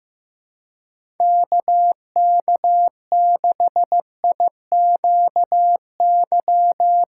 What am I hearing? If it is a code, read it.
Morse code, heard as a keyed tone, about 15 wpm, 700 Hz. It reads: KK6IQY
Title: USA Call Signs - 6 Characters - 2x Character Spacing - 15wpm